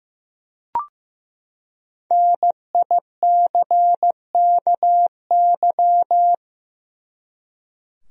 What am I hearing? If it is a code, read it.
Morse code, heard as a keyed tone, about 15 wpm, 700 Hz. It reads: NICKY
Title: USA Common Names - Max 5 Letters - 15wpm